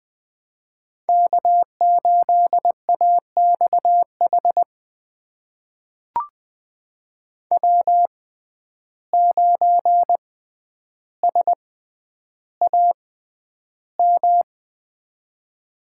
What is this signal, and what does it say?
Morse code, heard as a keyed tone, about 20 wpm, spaced as for 10 wpm, 700 Hz. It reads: K8AXH W9SAM
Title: USA Call Signs - 5 Characters - 6x Character Spacing - 20wpm